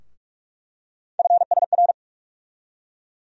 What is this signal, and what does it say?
Morse code, heard as a keyed tone, about 42 wpm, 700 Hz. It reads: FSR
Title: USA Call Sign Postfixes - 3 Letters 45wpm